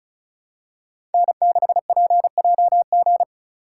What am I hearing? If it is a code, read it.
Morse code, heard as a keyed tone, about 36 wpm, 700 Hz. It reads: N6PJG
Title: USA Call Signs - 5 Characters - 3x Character Spacing - 35wpm